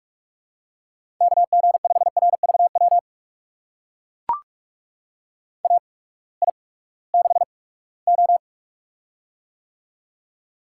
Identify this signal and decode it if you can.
Morse code, heard as a keyed tone, about 44 wpm, spaced as for 22 wpm, 700 Hz. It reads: KG5RVW AI6O
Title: USA Call Signs - 8x Character Spacing - 45wpm